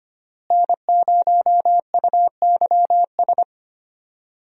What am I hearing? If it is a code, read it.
Morse code, heard as a keyed tone, about 25 wpm, 700 Hz. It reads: N0UYS
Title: USA Call Signs - 5 Characters - Repeated 3x - 25wpm